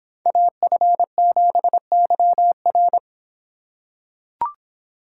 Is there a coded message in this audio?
Yes: Morse code from 0.3 to 3.0 s, heard as a keyed tone, about 26 wpm, 700 Hz. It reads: AF7YR